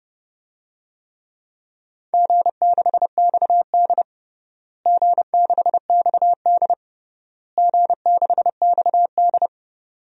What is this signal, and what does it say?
Morse code, heard as a keyed tone, about 30 wpm, 700 Hz. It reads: G6XD G6XD G6XD
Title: DXCC-100 Call Signs - 4 Characters - Repeated 3x - 30wpm